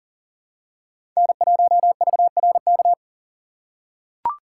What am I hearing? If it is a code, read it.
Morse code, heard as a keyed tone, about 40 wpm, 700 Hz. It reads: N1VRK
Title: USA Call Signs - 5 Characters - 4x Character Spacing - 40wpm